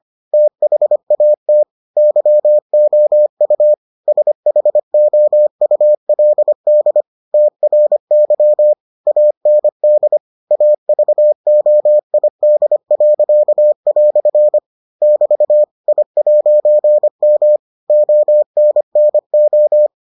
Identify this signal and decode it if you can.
Morse code, heard as a keyed tone, about 25 wpm, 600 Hz. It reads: THAT YOU SHOULD TRY AND AVOID." <BT> I'M ONNO